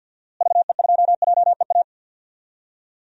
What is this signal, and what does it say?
Morse code, heard as a keyed tone, about 47 wpm, 700 Hz. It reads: VE2JEA
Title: DXCC-100 Call Signs - 6 Characters - Repeated 2x - 50wpm